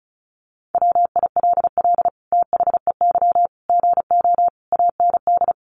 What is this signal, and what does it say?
Morse code, heard as a keyed tone, about 34 wpm, 700 Hz. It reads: WILL THEY GO AND
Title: Sentences from Top 1100 Words 35wpm